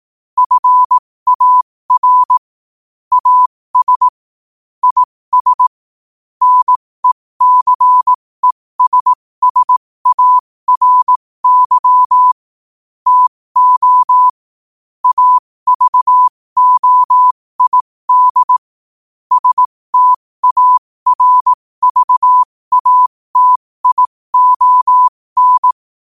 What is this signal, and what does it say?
Morse code, heard as a keyed tone, about 18 wpm, spaced as for 15 wpm, 1000 Hz. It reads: FAR AS IS NECESSARY TO AVOID STARVATION
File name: qotdfiles / 15wpm